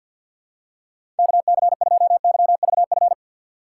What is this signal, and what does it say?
Morse code, heard as a keyed tone, about 48 wpm, 700 Hz. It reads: KC2YVF
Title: USA Call Signs - 6 Characters - 8x Character Spacing - 50wpm